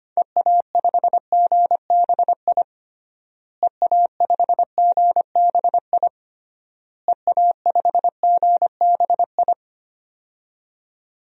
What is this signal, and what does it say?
Morse code, heard as a keyed tone, about 25 wpm, 700 Hz. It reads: EA5GBI EA5GBI EA5GBI
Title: DXCC-100 Call Signs - 6 Characters - Repeated 3x - 25wpm